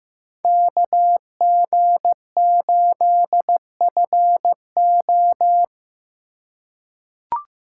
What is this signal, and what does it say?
Morse code, heard as a keyed tone, about 15 wpm, 700 Hz. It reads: KG8FO
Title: USA Call Signs - 5 Characters - 4x Character Spacing - 15wpm